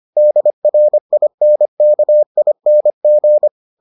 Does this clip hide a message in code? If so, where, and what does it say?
Yes: Morse code from 0.2 to 3.5 s, heard as a keyed tone, about 25 wpm, 600 Hz. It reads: DRINKING